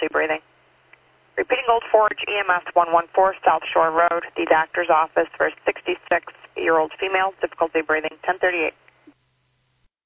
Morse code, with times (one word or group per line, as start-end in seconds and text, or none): none